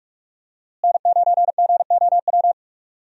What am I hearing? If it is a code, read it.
Morse code, heard as a keyed tone, about 46 wpm, 700 Hz. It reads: N9GOW
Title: USA Call Signs - 5 Characters - Repeated 2x - 45wpm